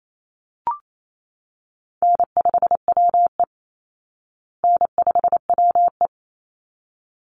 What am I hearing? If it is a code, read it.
Morse code, heard as a keyed tone, about 28 wpm, 700 Hz. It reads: N5WE N5WE
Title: Easy Call Signs Repeated 2x - 28wpm